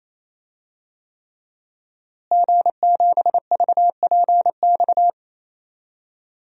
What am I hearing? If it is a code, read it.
Morse code, heard as a keyed tone, about 28 wpm, 700 Hz. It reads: G7VPX
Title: DXCC-100 Call Signs - 5 Characters - Repeated 3x - 28wpm